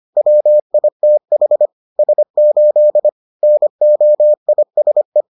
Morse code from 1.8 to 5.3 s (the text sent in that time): S8 NOISE